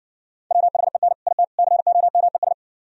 Morse code, heard as a keyed tone, about 59 wpm, 700 Hz. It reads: WHEN IT COMES